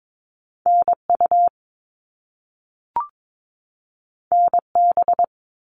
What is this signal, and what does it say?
Morse code, heard as a keyed tone, about 22 wpm, 700 Hz. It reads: NU NB
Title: Canadian Provinces and Territories 22wpm